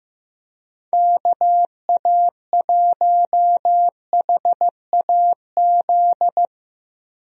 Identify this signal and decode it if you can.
Morse code, heard as a keyed tone, about 15 wpm, 700 Hz. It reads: KA1HAZ
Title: USA Call Signs - 6 Characters - Repeated 2x - 15wpm